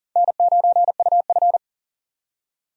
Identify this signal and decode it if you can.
Morse code, heard as a keyed tone, about 40 wpm, 700 Hz. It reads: N9UF
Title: USA Call Signs - 4 Characters - 40wpm